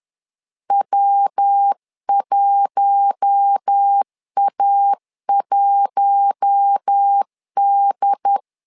Morse code, heard as a keyed tone, about 10 wpm, 800 Hz. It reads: W1A1D
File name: LEZIONI / 4 Q1 10wpm 2448